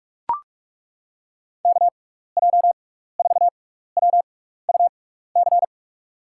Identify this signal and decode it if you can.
Morse code, heard as a keyed tone, about 42 wpm, spaced as for 15 wpm, 700 Hz. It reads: KJ4WUC
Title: USA Call Signs - 6x Character Spacing - 45wpm